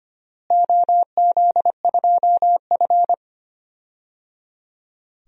Morse code, heard as a keyed tone, about 25 wpm, 700 Hz. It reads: OZ2F
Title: DXCC-100 Call Signs - 4 Characters - Repeated 3x - 25wpm